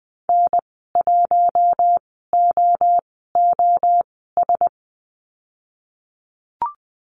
Morse code, heard as a keyed tone, about 19 wpm, spaced as for 14 wpm, 700 Hz. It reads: N1OOS
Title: 048 Lesson 9g - Number 4 - Call Signs 20wpm